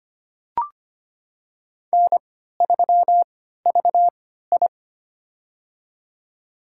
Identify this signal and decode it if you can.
Morse code, heard as a keyed tone, about 25 wpm, spaced as for 14 wpm, 700 Hz. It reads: N3VI
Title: USA Call Signs - 4 Characters - 3x Character Spacing - 25wpm